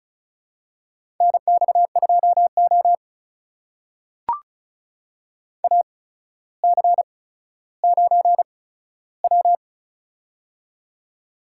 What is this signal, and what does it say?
Morse code, heard as a keyed tone, about 36 wpm, spaced as for 15 wpm, 700 Hz. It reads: NX2O AC9W
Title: USA Call Signs - 4 Characters - 8x Character Spacing - 35wpm